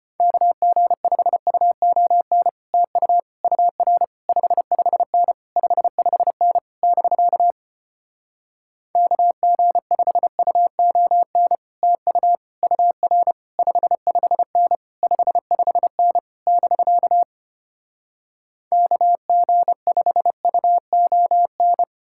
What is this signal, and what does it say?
Morse code, heard as a keyed tone, about 33 wpm, 700 Hz. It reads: KG5UON TU UR 55N 55N <BK> KG5UON TU UR 55N 55N <BK> KG5UON